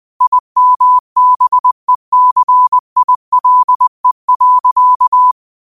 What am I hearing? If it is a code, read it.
Morse code, heard as a keyed tone, about 20 wpm, 1000 Hz. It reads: IMBECILE.
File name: qotdfiles / 20wpm